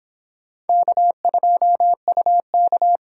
Morse code, heard as a keyed tone, about 28 wpm, 700 Hz. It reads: K2UK